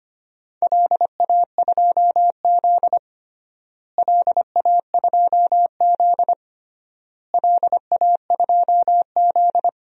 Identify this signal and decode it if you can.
Morse code, heard as a keyed tone, about 25 wpm, 700 Hz. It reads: LA2Z LA2Z LA2Z
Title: DXCC-100 Call Signs - 4 Characters - Repeated 3x - 25wpm